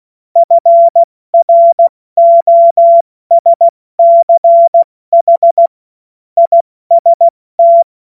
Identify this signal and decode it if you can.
Morse code, heard as a keyed tone, about 16 wpm, 680 Hz. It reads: FROSCH IST
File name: Morsekurs / Witze / Witz 08. November 2022 70 F80